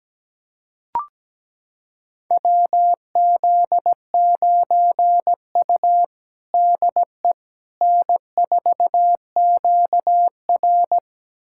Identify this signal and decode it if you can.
Morse code, heard as a keyed tone, about 17 wpm, 700 Hz. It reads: WZ9U DE N4QR
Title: USA Callsign DE Callsign - 4 Characters - 17wpm